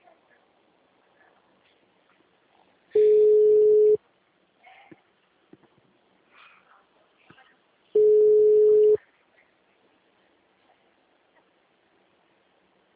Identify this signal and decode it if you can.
Ringback tone (Europe, 425 Hz).
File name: out- -2007- - - .63759